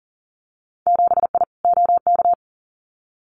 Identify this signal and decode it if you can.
Morse code, heard as a keyed tone, about 42 wpm, 700 Hz. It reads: 7I OK
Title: Winter Field Day Exchanges Encore 40wpm